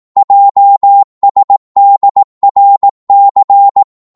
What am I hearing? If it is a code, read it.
Morse code, heard as a keyed tone, about 18 wpm, 800 Hz. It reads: JSDRC
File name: Morse / ltr-18wpm